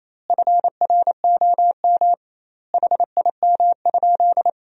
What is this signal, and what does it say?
Morse code, heard as a keyed tone, about 28 wpm, 700 Hz. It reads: FROM HIM?
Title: 2x Word-Spacing - Sentences from Top 300 Words - 28wpm